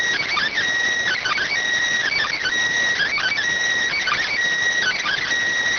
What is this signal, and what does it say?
Emergency Alert System (SAME) tones, heard as harsh buzzing bursts. It is heard from 1.1 to 5.2 s.